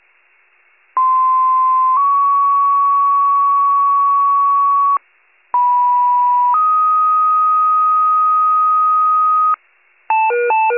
Two-tone page, heard as a one-second tone, then a three-second tone, about 1040 then 1120 Hz, about 980 then 1250 Hz. It is heard from 1.0 to 9.5 s.